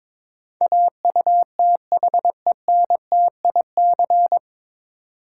Morse code, heard as a keyed tone, about 22 wpm, 700 Hz. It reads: AUTHENTIC
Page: Morse Code Ninja practice 22wpm